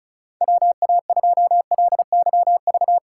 Morse code, heard as a keyed tone, about 34 wpm, 700 Hz. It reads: WA2LYV
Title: USA Call Signs - 6 Characters - 2x Character Spacing - 35wpm